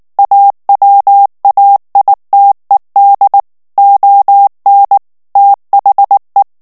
Morse code, heard as a keyed tone, about 19 wpm, 800 Hz. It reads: AWAITED ON THE
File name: morse mp3 / ANGLAIS texte clair 2 vitesse 18 mots minute